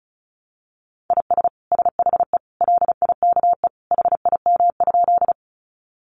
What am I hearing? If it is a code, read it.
Morse code, heard as a keyed tone, about 34 wpm, 700 Hz. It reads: IS SHE LIKE HIM?
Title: Sentences from Top 100 Words 35wpm